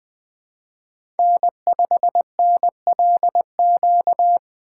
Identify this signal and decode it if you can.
Morse code, heard as a keyed tone, about 21 wpm, 700 Hz. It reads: N5NLQ
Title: USA Call Signs - 5 Characters - 3x Character Spacing - 20wpm